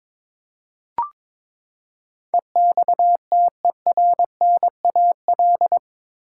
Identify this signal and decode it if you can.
Morse code, heard as a keyed tone, about 22 wpm, 700 Hz. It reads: EXTERNAL
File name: practice / 22wpm / Eight Letter Words - 22wpm